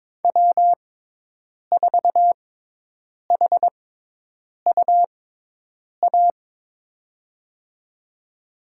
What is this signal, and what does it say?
Morse code, heard as a keyed tone, about 22 wpm, spaced as for 8 wpm, 700 Hz. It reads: W4HUA
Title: USA Call Signs - 5 Characters - 6x Character Spacing - 22wpm